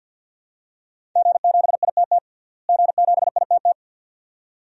Morse code, heard as a keyed tone, about 52 wpm, 700 Hz. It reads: G7ITT G7ITT
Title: DXCC-100 Call Signs - 5 Characters - Repeated 2x - 50wpm